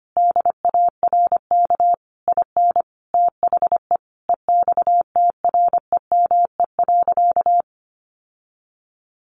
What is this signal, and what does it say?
Morse code, heard as a keyed tone, about 25 wpm, 700 Hz. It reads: DARK IN THE EXTREME.
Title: Sentences from Top 1300 Words 25wpm